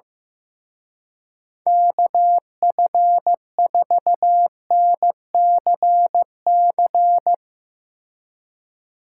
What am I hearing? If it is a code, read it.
Morse code, heard as a keyed tone, about 16 wpm, 700 Hz. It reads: KF4NCC
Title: Speed Racing - USA 2x3 Call Signs 15wpm